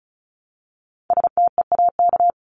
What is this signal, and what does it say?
Morse code, heard as a keyed tone, about 34 wpm, 700 Hz. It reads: STEAK
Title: Top 1251-1500 Words 35wpm